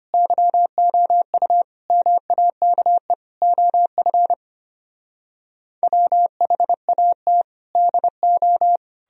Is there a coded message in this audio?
Yes: Morse code from 0.1 to 8.8 s, heard as a keyed tone, about 33 wpm, 700 Hz. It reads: YOU MAKE OF WHAT DO